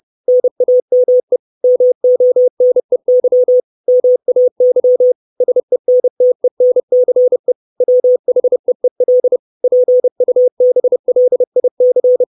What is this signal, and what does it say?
Morse code, heard as a keyed tone, about 30 wpm, 500 Hz. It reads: NAME MONEY MAY SENTENCE WHEEL PUBLIC
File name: 632 Words 30WPM